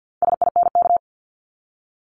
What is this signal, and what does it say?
Morse code, heard as a keyed tone, about 50 wpm, 690 Hz. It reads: SINK
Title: Multi-letter Phonemes Repeated 3x 50wpm